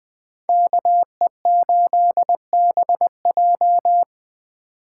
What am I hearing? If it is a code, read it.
Morse code, heard as a keyed tone, about 23 wpm, 700 Hz. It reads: KE8BJ
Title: Speed Racing - USA 2x2 Call Signs 20wpm